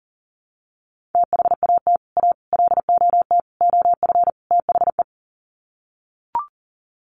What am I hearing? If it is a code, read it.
Morse code, heard as a keyed tone, about 40 wpm, 700 Hz. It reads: THAT A LOT OF THE
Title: Sets of 5 Words Encore2 40wpm